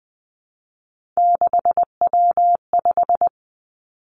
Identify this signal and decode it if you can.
Morse code, heard as a keyed tone, about 19 wpm, 700 Hz. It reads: 6W5
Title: Three Challenging Letter-Number Combinations 20wpm